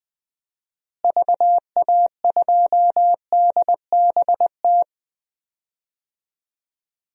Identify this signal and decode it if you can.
Morse code, heard as a keyed tone, about 19 wpm, 700 Hz. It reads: VA2DBT
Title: DXCC-100 Call Signs - 6 Characters - Repeated 2x - 20wpm